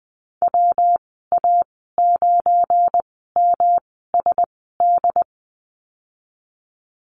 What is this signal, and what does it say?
Morse code, heard as a keyed tone, about 20 wpm, spaced as for 14 wpm, 700 Hz. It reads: WA9MSD
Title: 193 Lesson 31g - Letter v - Call Signs 20wpm